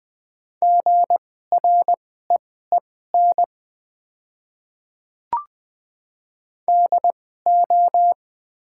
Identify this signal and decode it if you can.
Morse code, heard as a keyed tone, about 20 wpm, spaced as for 14 wpm, 700 Hz. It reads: GREEN DO
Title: 2x Character Spacing - Top 500 Words - 20wpm